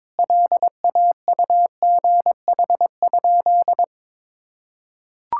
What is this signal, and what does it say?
Morse code, heard as a keyed tone, about 22 wpm, 700 Hz. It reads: LAUGH?
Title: Sentences from Top 1000 Words - Review Part 5 - 22wpm